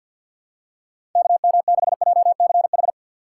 Morse code, heard as a keyed tone, about 50 wpm, 700 Hz. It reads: KM6JCH
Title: USA Call Signs - 2x Character Spacing - 50wpm